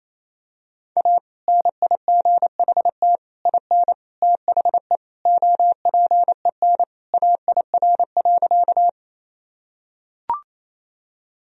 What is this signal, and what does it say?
Morse code, heard as a keyed tone, about 28 wpm, 700 Hz. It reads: A NIGHT IN THE OPEN AIR.
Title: Sentences from Top 1000 Words - Review Part 1 - 28wpm